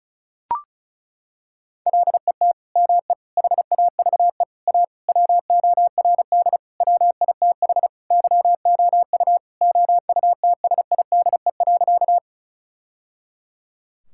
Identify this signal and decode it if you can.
Morse code, heard as a keyed tone, about 34 wpm, 700 Hz. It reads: LET ME HAVE A WORD WITH YOU OUTSIDE.
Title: Sentences from Top 1000 Words - Review Part 6 - 35wpm